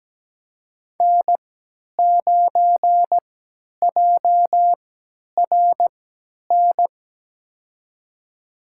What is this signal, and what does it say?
Morse code, heard as a keyed tone, about 17 wpm, spaced as for 10 wpm, 700 Hz. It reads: N9JRN
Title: USA Call Signs - 5 Characters - 3x Character Spacing - 17wpm